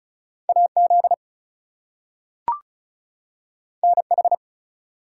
Morse code, heard as a keyed tone, about 36 wpm, 700 Hz. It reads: AZ NH
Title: US States and Canadian Provinces 35wpm